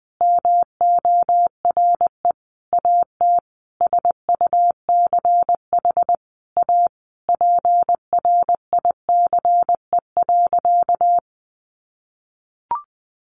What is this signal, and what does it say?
Morse code, heard as a keyed tone, about 20 wpm, 700 Hz. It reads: MORE AT SUCH A PRICE.
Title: Sentences from Top 1100 Words 20wpm